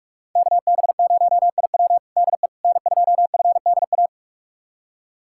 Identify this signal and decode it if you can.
Morse code, heard as a keyed tone, about 43 wpm, 690 Hz. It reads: KB0IW DE N2FDA
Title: USA Callsign DE Callsign 45wpm